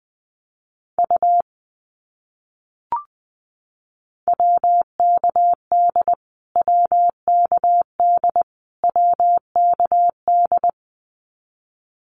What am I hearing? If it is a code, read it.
Morse code, heard as a keyed tone, about 19 wpm, 700 Hz. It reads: U WKD WKD WKD
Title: Abbreviations Repeated 3x - 20wpm